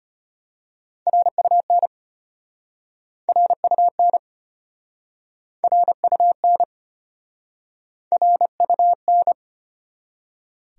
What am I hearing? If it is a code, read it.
Morse code, heard as a keyed tone, about 34 wpm, 700 Hz. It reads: RUN RUN RUN RUN